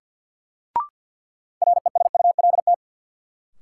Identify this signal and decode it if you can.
Morse code, heard as a keyed tone, about 53 wpm, 700 Hz. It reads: RESULT
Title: Sets of 3 Words - New Word Drill - 50wpm